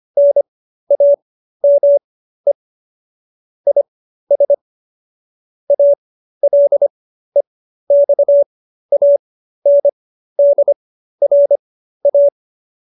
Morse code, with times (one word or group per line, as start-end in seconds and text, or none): 0.2-2.5 NAME
3.7-4.5 IS
5.7-12.3 ALEXANDRA